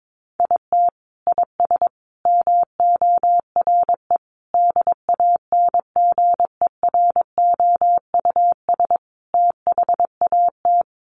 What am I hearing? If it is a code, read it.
Morse code, heard as a keyed tone, about 22 wpm, 700 Hz. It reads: IT IS MORE DANGEROUS THAT
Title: Sentences from Top 1200 Words 22wpm